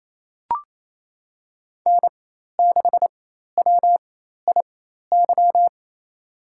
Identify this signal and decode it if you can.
Morse code, heard as a keyed tone, about 28 wpm, spaced as for 13 wpm, 700 Hz. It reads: N6WIY
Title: USA Call Signs - 5 Characters - 4x Character Spacing - 28wpm